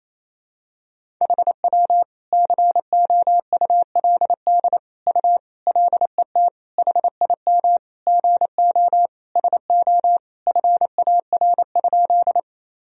Morse code, heard as a keyed tone, about 28 wpm, 700 Hz. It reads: HW COULD U LET HIM GO SO FAR?
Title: Sentences from Top 300 Words and Abbreviations 28wpm